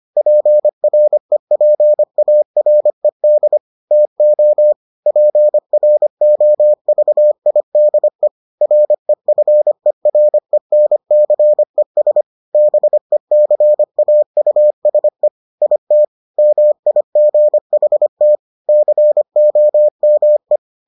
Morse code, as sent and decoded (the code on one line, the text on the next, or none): .--. .-. .--. .- .-. -.. / - --- / .--. .-. --- ...- .. -.. / .-. ..-. .-. -. -.-. ... / -... -.-. .- ..- ... / .. - / -- .. --. .... - / -.-. --- -- .
PREPARED TO PROVIDE REFERENCES BECAUSE IT MIGHT COME